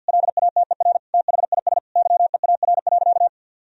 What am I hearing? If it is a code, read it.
Morse code, heard as a keyed tone, about 50 wpm, 700 Hz. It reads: LATER THIS YEAR.
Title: Sentences from Top 1000 Words - Review Part 3 - 50wpm